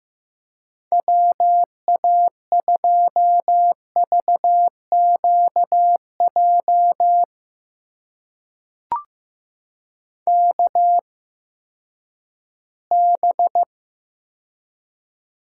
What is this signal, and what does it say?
Morse code, heard as a keyed tone, about 15 wpm, spaced as for 10 wpm, 700 Hz. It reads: WA2VQJ KB
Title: USA Call Signs - 6 Characters - 8x Character Spacing - 15wpm